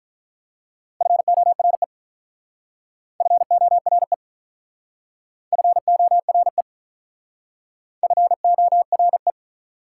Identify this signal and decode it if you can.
Morse code, heard as a keyed tone, about 45 wpm, 700 Hz. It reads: FORE FORE FORE FORE